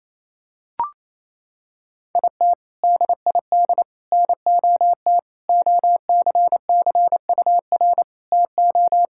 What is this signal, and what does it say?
Morse code, heard as a keyed tone, about 27 wpm, 700 Hz. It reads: IT DID NOT OCCUR TO
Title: Sentences from Top 1000 Words - Review Part 6 - 28wpm